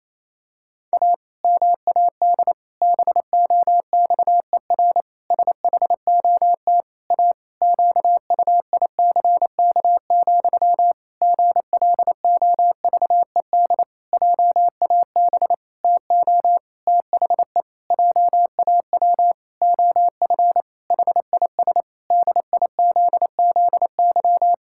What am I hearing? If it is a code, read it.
Morse code, heard as a keyed tone, about 28 wpm, 700 Hz. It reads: A MAD BOXER SHOT A QUICK, GLOVED JAB TO THE JAW OF HIS DIZZY
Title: Pangram Challenge v2 - 28wpm